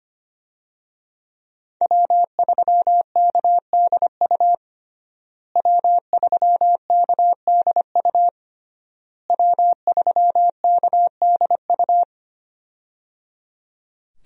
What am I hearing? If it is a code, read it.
Morse code, heard as a keyed tone, about 25 wpm, 700 Hz. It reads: W3KDU W3KDU W3KDU
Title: USA Call Signs - 5 Characters - Repeated 3x - 25wpm